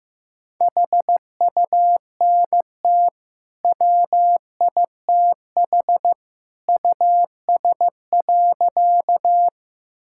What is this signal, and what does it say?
Morse code, heard as a keyed tone, about 16 wpm, 700 Hz. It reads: HUNT WITH US.
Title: Sentences from Top 1600 Words 15wpm